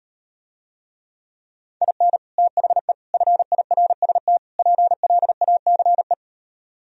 Morse code, heard as a keyed tone, about 38 wpm, 700 Hz. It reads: IN THE FIRST PLACE